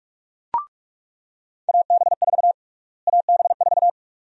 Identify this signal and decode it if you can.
Morse code, heard as a keyed tone, about 46 wpm, 700 Hz. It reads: AB4 AB4
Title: USA Call Sign Prefixes - Repeated 2x - 45wpm